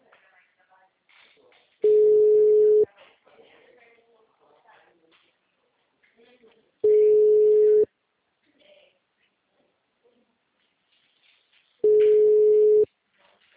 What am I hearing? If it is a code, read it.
Ringback tone (Europe, 425 Hz), with speech.